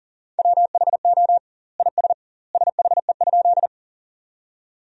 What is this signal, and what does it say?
Morse code, heard as a keyed tone, about 40 wpm, 700 Hz. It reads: WHO IS SHE?
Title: 2x Word-Spacing - Sentences from Top 100 Words - 40wpm